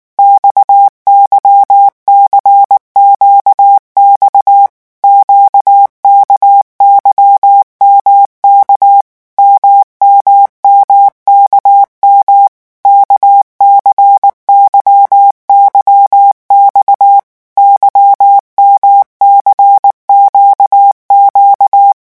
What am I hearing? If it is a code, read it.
Morse code, heard as a keyed tone, about 19 wpm, 800 Hz. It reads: XYCQX QKYMK MMMKM KCYYX YMCQQ
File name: revision1-vitesse 18 mots